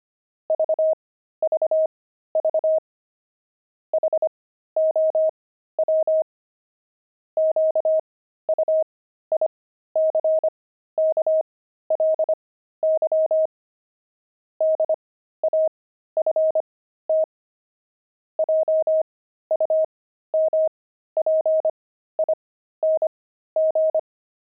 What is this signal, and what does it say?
Morse code, heard as a keyed tone, about 25 wpm, spaced as for 13 wpm, 630 Hz. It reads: VVV HOW QUICKLY DAFT JUMPING
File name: pangram-fs-13-cs-25-daft